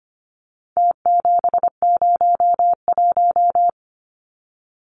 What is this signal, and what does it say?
Morse code, heard as a keyed tone, about 25 wpm, 700 Hz. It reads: T701
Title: Numbers - Contest Style - 4-Digit 25wpm